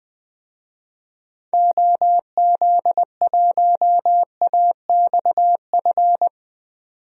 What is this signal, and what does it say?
Morse code, heard as a keyed tone, about 21 wpm, 700 Hz. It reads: OZ1AXF
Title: DXCC-100 Call Signs - 6 Characters - 20wpm